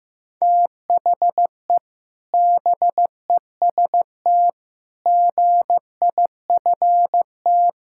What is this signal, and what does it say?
Morse code, heard as a keyed tone, about 16 wpm, 700 Hz. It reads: THE BEST GIFT
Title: Sentences from Top 1000 Words - Review Part 6 - 15wpm